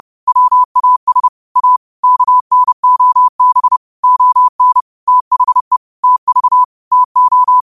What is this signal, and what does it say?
Morse code, heard as a keyed tone, about 30 wpm, 1000 Hz. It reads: WAS A KNOB ON THE TV TO
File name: qotdfiles / 30wpm